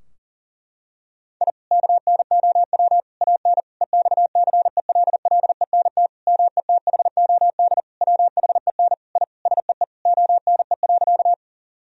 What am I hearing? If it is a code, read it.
Morse code, heard as a keyed tone, about 40 wpm, 700 Hz. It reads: I KNOW AN EXCELLENT METHOD WHEN I SEE ONE.
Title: Sentences from Top 1000 Words - Review Part 3 - 40wpm